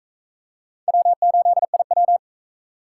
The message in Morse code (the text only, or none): W8IW